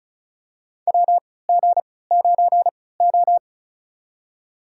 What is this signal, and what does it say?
Morse code, heard as a keyed tone, about 36 wpm, spaced as for 20 wpm, 700 Hz. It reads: WG9O
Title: USA Call Signs - 3x Character Spacing - 35wpm